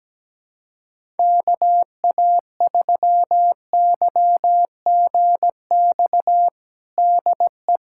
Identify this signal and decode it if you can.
Morse code, heard as a keyed tone, about 17 wpm, 700 Hz. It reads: KA3YGX DE
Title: USA Callsign DE Callsign 17wpm